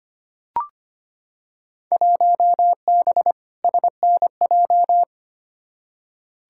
Morse code, heard as a keyed tone, about 25 wpm, 700 Hz. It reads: 1B SNJ
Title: Field Day Exchanges - 25wpm